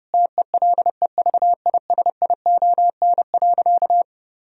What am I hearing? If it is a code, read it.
Morse code, heard as a keyed tone, about 32 wpm, 700 Hz. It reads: TELEVISION.
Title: Sentences from Top 1000 Words - Review Part 4 - 30wpm